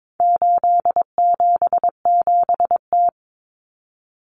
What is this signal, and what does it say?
Morse code, heard as a keyed tone, about 22 wpm, 700 Hz. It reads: 877T
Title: Common Cut Numbers - 4-Digit 22wpm